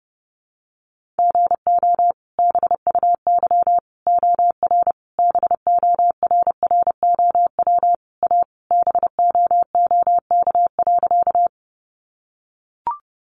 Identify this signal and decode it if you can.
Morse code, heard as a keyed tone, about 30 wpm, 700 Hz. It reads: GO BUY OR BORROW A BOOK.
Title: Sentences from Top 1500 Words 30wpm